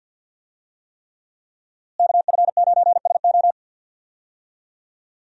Morse code, heard as a keyed tone, about 53 wpm, 700 Hz. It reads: KF9SO
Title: USA Call Signs - 5 Characters - Repeated 3x - 50wpm